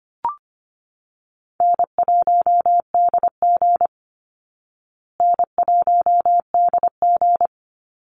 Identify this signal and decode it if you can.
Morse code, heard as a keyed tone, about 25 wpm, 700 Hz. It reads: N1DG N1DG
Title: Easy Call Signs Repeated 3x - 25wpm